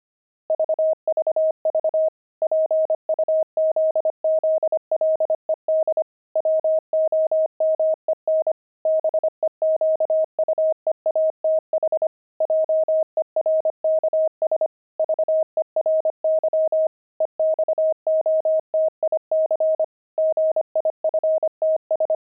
Morse code, heard as a keyed tone, about 25 wpm, 630 Hz. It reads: VVV PUZZLED WOMEN BEQUEATH JERKS VERY EXOTIC GIFTS
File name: pangram-fs-25-cs-25-jerks